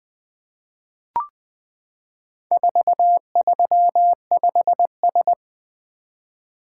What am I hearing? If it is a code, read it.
Morse code, heard as a keyed tone, about 20 wpm, 700 Hz. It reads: 435S
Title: Four Letter-Number Combinations - 20wpm